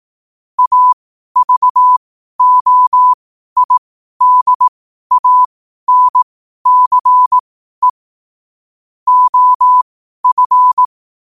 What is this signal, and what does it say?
Morse code, heard as a keyed tone, about 18 wpm, spaced as for 12 wpm, 1000 Hz. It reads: AVOIDANCE OF
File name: qotdfiles / 12wpm